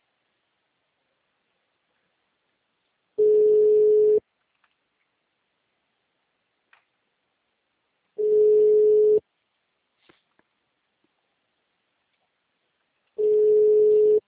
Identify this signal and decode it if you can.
Ringback tone (Europe, 425 Hz).
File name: out- -2012- - - .4